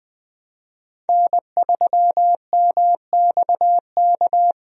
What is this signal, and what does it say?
Morse code, heard as a keyed tone, about 20 wpm, 700 Hz. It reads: N3MXK
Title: USA Call Signs - 5 Characters - 20wpm